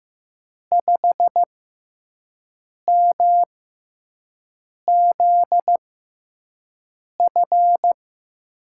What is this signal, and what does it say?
Morse code, heard as a keyed tone, about 15 wpm, 700 Hz. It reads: 5 M Z F
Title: USA Call Signs - 5 Characters - 6x Character Spacing - 15wpm